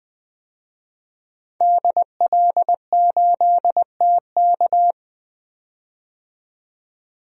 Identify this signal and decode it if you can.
Morse code, heard as a keyed tone, about 21 wpm, 700 Hz. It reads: DL8TK
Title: DXCC-100 Call Signs - 5 Characters - 20wpm